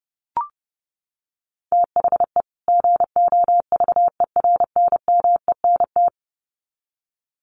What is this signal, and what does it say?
Morse code, heard as a keyed tone, about 30 wpm, 700 Hz. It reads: THE GOVERNMENT
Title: Sets of 2 Words 30wpm